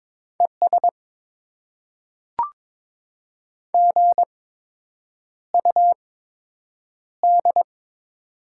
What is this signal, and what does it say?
Morse code, heard as a keyed tone, about 22 wpm, spaced as for 8 wpm, 700 Hz. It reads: ES GUD
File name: Top 20 QSO Elements - 8x Character Spacing - 22wpm